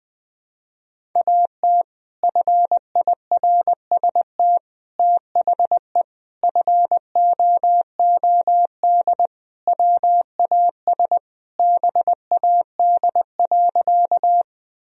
Morse code, heard as a keyed tone, about 19 wpm, 700 Hz. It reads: AT FIRST THE FOOD WAS BAD.
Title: Sentences from Top 1000 Words - Review Part 1 - 20wpm